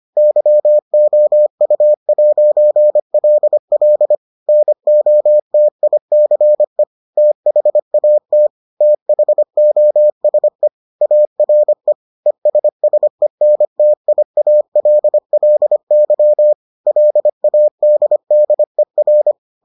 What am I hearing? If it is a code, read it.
Morse code, heard as a keyed tone, about 25 wpm, 600 Hz. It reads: YOU'LL NOTICE THAT THOSE ARE ESSENTIALLY LADDER